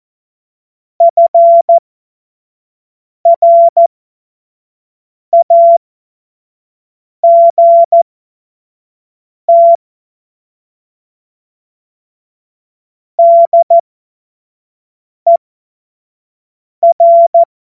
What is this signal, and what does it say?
Morse code, heard as a keyed tone, about 14 wpm, spaced as for 5 wpm, 680 Hz. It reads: FRAGT DER
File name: Morsekurs / Witze / Witz 18. April 2023 25 F70